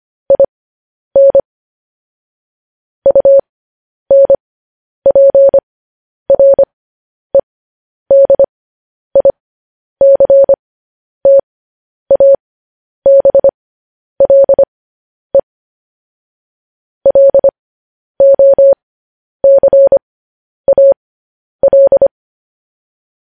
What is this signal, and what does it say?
Morse code, heard as a keyed tone, about 25 wpm, spaced as for 10 wpm, 550 Hz. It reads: IN UNPREDICTABLE LOCAL
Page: news headlines in morse code at 10 wpm